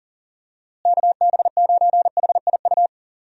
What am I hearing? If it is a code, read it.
Morse code, heard as a keyed tone, about 40 wpm, 700 Hz. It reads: KB9HIU
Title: USA Call Signs - 6 Characters - Repeated 3x - 40wpm